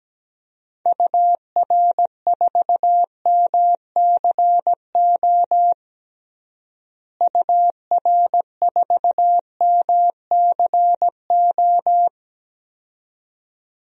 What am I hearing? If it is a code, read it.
Morse code, heard as a keyed tone, about 17 wpm, 700 Hz. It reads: UR4MCO UR4MCO
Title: DXCC-100 Call Signs - 6 Characters - Repeated 2x - 17wpm